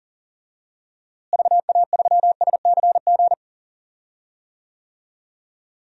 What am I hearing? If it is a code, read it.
Morse code, heard as a keyed tone, about 40 wpm, 700 Hz. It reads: VA3SCG
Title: DXCC-100 Call Signs - 6 Characters - Repeated 2x - 40wpm